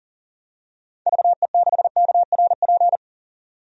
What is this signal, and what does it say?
Morse code, heard as a keyed tone, about 40 wpm, 700 Hz. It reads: VE6KRP
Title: DXCC-100 Call Signs - 6 Characters - 40wpm